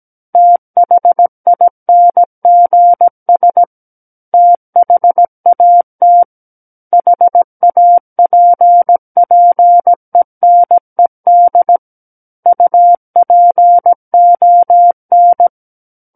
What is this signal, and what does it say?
Morse code, heard as a keyed tone, about 17 wpm, 700 Hz. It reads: THINGS THAT HAPPENED UPON
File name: Book One / 08-Chapter 8 - 17 WPM